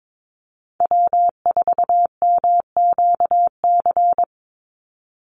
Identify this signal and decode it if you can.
Morse code, heard as a keyed tone, about 22 wpm, 700 Hz. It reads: W4MQC
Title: Call Signs - 22wpm